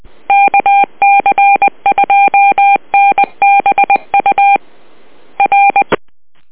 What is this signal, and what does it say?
Morse code, heard as a keyed tone, about 20 wpm, 800 Hz. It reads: KC2NBU R